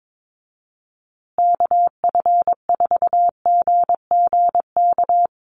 Morse code, heard as a keyed tone, about 22 wpm, 700 Hz. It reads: KF4GGK
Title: USA Call Signs Repeated 2x 22wpm